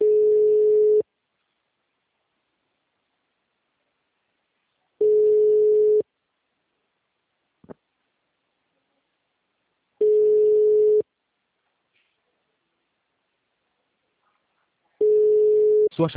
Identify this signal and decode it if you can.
Ringback tone (Europe, 425 Hz), with speech.
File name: out- -2024- - - .7766